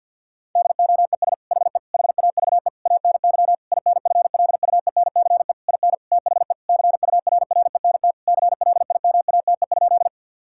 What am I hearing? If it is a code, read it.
Morse code, heard as a keyed tone, about 50 wpm, 700 Hz. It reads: DOES HE HAVE ANY INFLUENCE IN THE CURRENT CLIMATE?
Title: Sentences from Top 1600 Words 50wpm